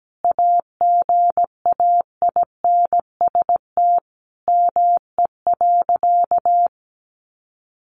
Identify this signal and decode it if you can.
Morse code, heard as a keyed tone, about 17 wpm, 700 Hz. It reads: AGAINST ME.
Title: Sentences from Top 1100 Words 17wpm